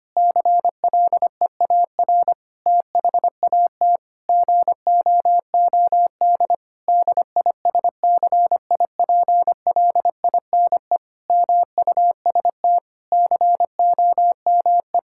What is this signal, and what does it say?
Morse code, heard as a keyed tone, about 25 wpm, 700 Hz. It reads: CLEAR THAT GOOD DISCIPLINE MUST COME
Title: Sentences from Top 1000 Words - Review Part 6 - 25wpm